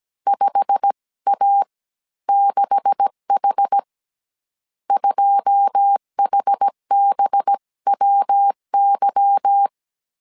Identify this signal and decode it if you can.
Morse code, heard as a keyed tone, about 17 wpm, spaced as for 14 wpm, 800 Hz. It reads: 5A6H 2HBWY
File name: CW / DET / 1 9 1500C S C 16wpm 2909